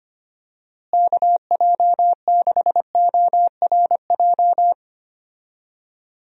Morse code, heard as a keyed tone, about 25 wpm, 700 Hz. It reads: KJ6ORJ
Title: USA Call Signs - 6 Characters - Repeated 2x - 25wpm